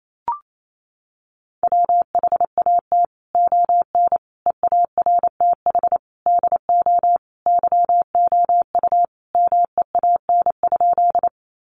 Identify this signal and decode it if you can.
Morse code, heard as a keyed tone, about 28 wpm, 700 Hz. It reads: WHAT ON EARTH DO YOU MEAN?
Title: Sentences from Top 200 Words 28wpm